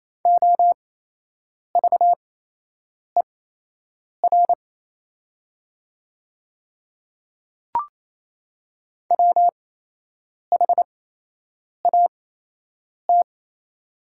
Morse code, heard as a keyed tone, about 28 wpm, spaced as for 8 wpm, 700 Hz. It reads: OVER WHAT
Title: 8x Character Spacing - Top 200 Words - 28wpm